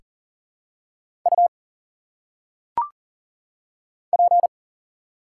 Morse code, heard as a keyed tone, about 40 wpm, 700 Hz. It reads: U P
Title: Single Letters - Phonetic Pronunciation 40wpm